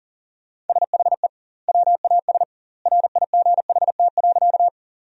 Morse code, heard as a keyed tone, about 40 wpm, 700 Hz. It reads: SHE WAS RIGHT.
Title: 2x Word-Spacing - Sentences from Top 200 Words - 40wpm